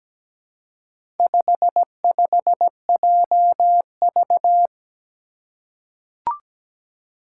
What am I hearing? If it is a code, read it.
Morse code, heard as a keyed tone, about 17 wpm, 700 Hz. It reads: 55JV
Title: Four Challenging Letter-Number Combinations 17wpm